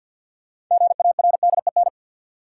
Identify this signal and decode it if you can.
Morse code, heard as a keyed tone, about 53 wpm, 700 Hz. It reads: GARDEN
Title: Sentences from Top 700 Words - New Word Drill - 50wpm